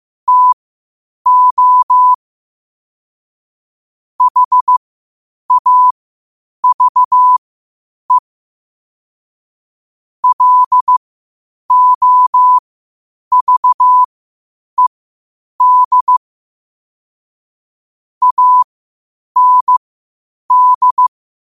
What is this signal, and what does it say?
Morse code, heard as a keyed tone, about 15 wpm, spaced as for 8 wpm, 1000 Hz. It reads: TO HAVE LOVED AND
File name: qotdfiles / 8wpm